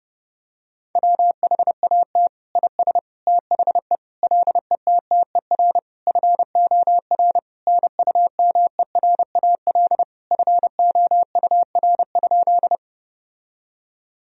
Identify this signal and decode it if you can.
Morse code, heard as a keyed tone, about 30 wpm, 700 Hz. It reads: WHAT IS THE LETTER FOR NUMERAL FOUR?
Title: Sentences from Top 1000 Words - Review Part 2 - 30wpm